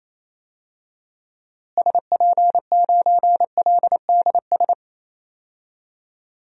Morse code, heard as a keyed tone, about 28 wpm, 700 Hz. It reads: SP9LDS
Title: DXCC-100 Call Signs - 6 Characters - Repeated 2x - 28wpm